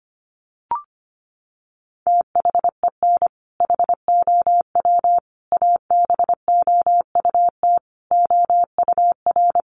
Morse code, heard as a keyed tone, about 25 wpm, 700 Hz. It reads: THEN HOW ABOUT OUR
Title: Sentences from Top 900 Words 25wpm